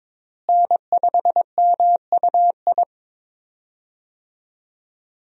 Morse code, heard as a keyed tone, about 22 wpm, 700 Hz. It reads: N5MUI
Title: USA Call Signs - 5 Characters - Repeated 2x - 22wpm